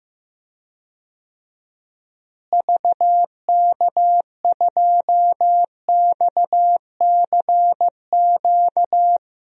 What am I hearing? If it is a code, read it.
Morse code, heard as a keyed tone, about 15 wpm, 700 Hz. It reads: VK2XCQ
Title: DXCC-100 Call Signs - 6 Characters - Repeated 3x - 15wpm